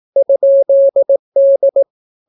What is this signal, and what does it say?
Morse code, heard as a keyed tone, about 18 wpm, 550 Hz. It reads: ?D